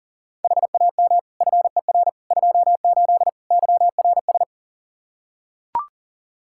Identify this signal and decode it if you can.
Morse code, heard as a keyed tone, about 40 wpm, 700 Hz. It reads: HAM FER 28 YRS
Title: Practice QSO - Exchange 3 - More Info - 40wpm